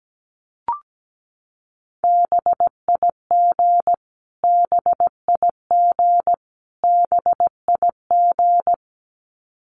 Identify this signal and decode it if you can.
Morse code, heard as a keyed tone, about 17 wpm, 700 Hz. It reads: BIG BIG BIG
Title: Top 200 Words Repeated 17wpm